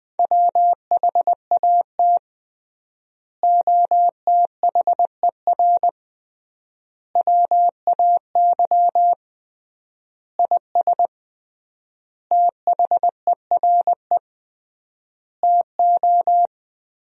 Morse code, heard as a keyed tone, about 20 wpm, 700 Hz. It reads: WHAT OTHER WAY IS THERE TO
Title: 3x Word-Spacing - Sentences from Top 100 Words - 20wpm